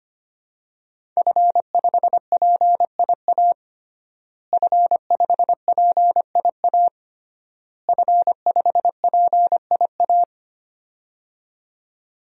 Morse code, heard as a keyed tone, about 25 wpm, 700 Hz. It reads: F5PIA F5PIA F5PIA
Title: DXCC-100 Call Signs - 5 Characters - Repeated 3x - 25wpm